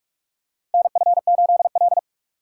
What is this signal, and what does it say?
Morse code, heard as a keyed tone, about 46 wpm, 700 Hz. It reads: NF8L
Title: USA Call Signs - 3x Character Spacing - 45wpm